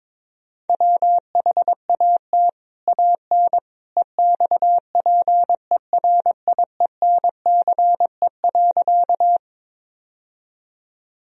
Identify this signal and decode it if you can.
Morse code, heard as a keyed tone, about 22 wpm, 700 Hz. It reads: WHAT AN EXPERIENCE.
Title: Sentences from Top 1000 Words - Review Part 3 - 22wpm